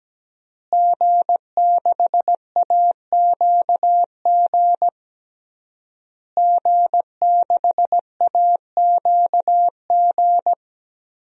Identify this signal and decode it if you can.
Morse code, heard as a keyed tone, about 17 wpm, 700 Hz. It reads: G6AQG G6AQG
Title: DXCC-100 Call Signs - 5 Characters - Repeated 3x - 17wpm